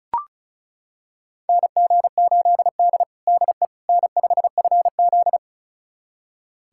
Morse code, heard as a keyed tone, about 36 wpm, 700 Hz. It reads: NG8D DE N5FZ
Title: USA Callsign DE Callsign - 4 Characters - 35wpm